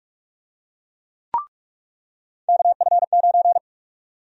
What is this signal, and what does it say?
Morse code, heard as a keyed tone, about 46 wpm, 700 Hz. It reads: KF9
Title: USA Call Sign Prefixes - 45wpm